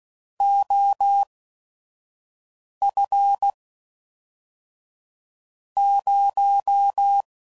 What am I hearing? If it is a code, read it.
Morse code, heard as a keyed tone, about 16 wpm, spaced as for 5 wpm, 800 Hz. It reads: OF 0
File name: CW Code Groups @5WPM / Technical Subjects / Gates Intro